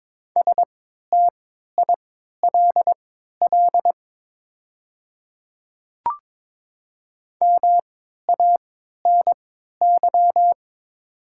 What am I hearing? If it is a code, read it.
Morse code, heard as a keyed tone, about 23 wpm, spaced as for 13 wpm, 700 Hz. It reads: STILL MANY
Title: 3x Character Spacing - Top 300 Words - 22wpm